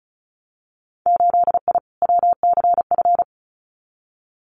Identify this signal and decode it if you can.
Morse code, heard as a keyed tone, about 36 wpm, 700 Hz. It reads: 8I WCF
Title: Winter Field Day Exchanges Encore 35wpm